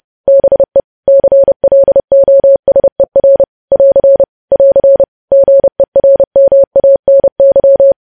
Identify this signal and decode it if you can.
Morse code, heard as a keyed tone, about 30 wpm, 550 Hz. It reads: BE CLOSER <AR> <AR> GERMANY